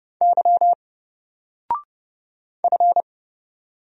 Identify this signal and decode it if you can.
Morse code, heard as a keyed tone, about 31 wpm, 700 Hz. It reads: Y F
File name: Single-Letter-Number-30wpm